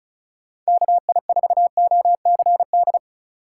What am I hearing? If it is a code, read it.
Morse code, heard as a keyed tone, about 34 wpm, 700 Hz. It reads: KI4OCD
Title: USA Call Signs - 6 Characters - 6x Character Spacing - 35wpm